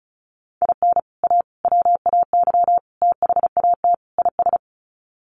Morse code, heard as a keyed tone, about 34 wpm, 700 Hz. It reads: IN A WAY THAT IS
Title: Sets of 5 Words Encore 35wpm